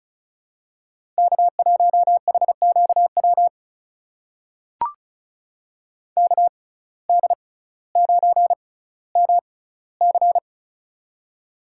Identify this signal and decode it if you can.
Morse code, heard as a keyed tone, about 36 wpm, spaced as for 18 wpm, 700 Hz. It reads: K1HQW KD9MC
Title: USA Call Signs - 5 Characters - 6x Character Spacing - 35wpm